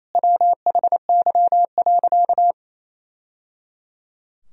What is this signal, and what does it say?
Morse code, heard as a keyed tone, about 28 wpm, 700 Hz. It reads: WHY.
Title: Sentences from Top 1000 Words - Review Part 3 - 28wpm